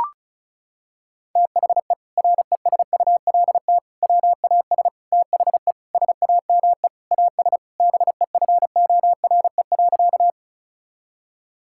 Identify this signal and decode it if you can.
Morse code, heard as a keyed tone, about 34 wpm, 700 Hz. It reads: THE RESULT WAS THE SAME AS BEFORE.
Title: Sentences from Top 1000 Words - Review Part 5 - 35wpm